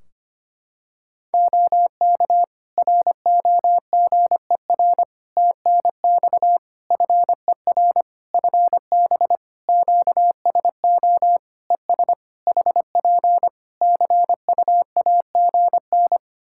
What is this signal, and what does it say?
Morse code, heard as a keyed tone, about 25 wpm, 700 Hz. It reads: OK ROGER TNX FER FB QSO ES HP CUAGN
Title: Practice QSO - Exchange 4 - Ending - 25wpm